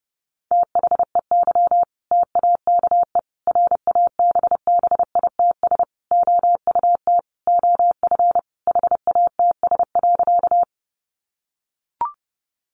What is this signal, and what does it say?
Morse code, heard as a keyed tone, about 30 wpm, 700 Hz. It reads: THEY TAKE RABBITS OUT OF HATS.
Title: Easy Phrases 30wpm